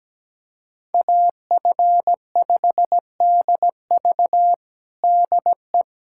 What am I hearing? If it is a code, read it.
Morse code, heard as a keyed tone, about 17 wpm, 700 Hz. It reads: AF5DV DE
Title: USA Callsign DE Callsign - 5 Characters - 17wpm